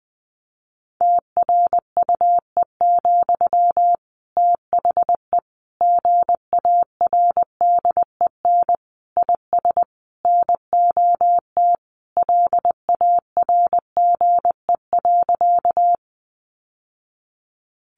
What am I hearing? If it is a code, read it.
Morse code, heard as a keyed tone, about 20 wpm, 700 Hz. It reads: TRUE, THE GARDEN IS NOT LARGE.
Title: Sentences from Top 700 Words 20wpm